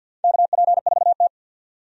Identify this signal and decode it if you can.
Morse code, heard as a keyed tone, about 53 wpm, 700 Hz. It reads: KP4T
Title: DXCC-100 Call Signs - 4 Characters - 50wpm